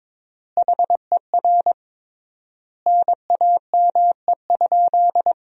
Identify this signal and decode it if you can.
Morse code, heard as a keyed tone, about 22 wpm, 700 Hz. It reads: HER NAME?
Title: 3x Word-Spacing - Sentences from Top 200 Words - 22wpm